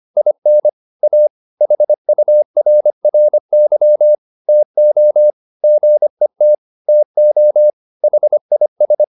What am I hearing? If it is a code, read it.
Morse code, heard as a keyed tone, about 25 wpm, 600 Hz. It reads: IN A HURRY TO GET TO HIS